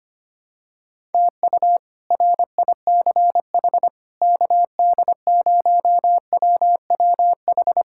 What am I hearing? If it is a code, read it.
Morse code, heard as a keyed tone, about 25 wpm, 700 Hz. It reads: TU RICH KD0WWH
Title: K1USN SST QSO Practice - 25wpm